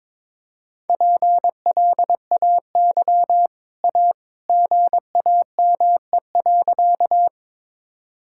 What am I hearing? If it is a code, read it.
Morse code, heard as a keyed tone, about 22 wpm, 690 Hz. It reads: PLAY A GAME.
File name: practice / 22wpm / Easy Sentences with Hints - 22wpm